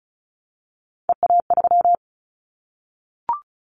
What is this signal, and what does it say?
Morse code, heard as a keyed tone, about 34 wpm, 700 Hz. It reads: EA3
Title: Callsign Prefixes for Top 25 DXCC - Repeated 3x - 35wpm